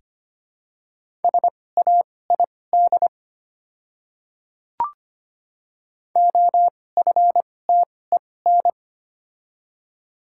Morse code, heard as a keyed tone, about 25 wpm, spaced as for 18 wpm, 700 Hz. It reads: SAID OFTEN
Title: 2x Character Spacing - Top 300 Words - 25wpm